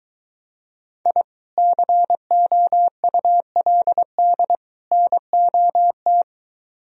Morse code, heard as a keyed tone, about 22 wpm, 700 Hz. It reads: I COULD NOT